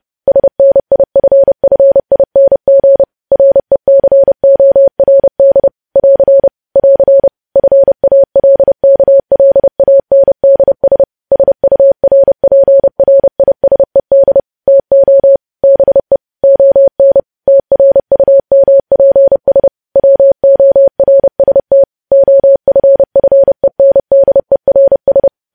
Morse code, heard as a keyed tone, about 30 wpm, 550 Hz. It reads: SNIFFING RECORD <AR> <AR> FALKLANDS SURPRISED TO BE ON TRUMPS WORST OFFENDERS
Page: news headlines in morse code at 30 wpm